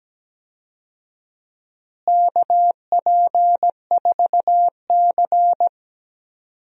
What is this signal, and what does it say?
Morse code, heard as a keyed tone, about 17 wpm, 700 Hz. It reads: KP4C
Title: DXCC-100 Call Signs - 4 Characters - Repeated 3x - 17wpm